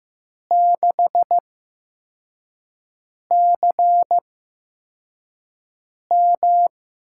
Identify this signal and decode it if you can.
Morse code, heard as a keyed tone, about 15 wpm, spaced as for 4 wpm, 700 Hz. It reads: 6CM
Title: USA Call Signs - 6 Characters - 8x Character Spacing - 15wpm